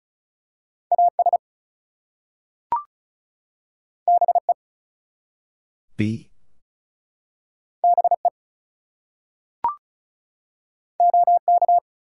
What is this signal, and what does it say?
Morse code, heard as a keyed tone, about 36 wpm, 700 Hz. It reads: AS BE BE OK
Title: Two Letter Words - 35wpm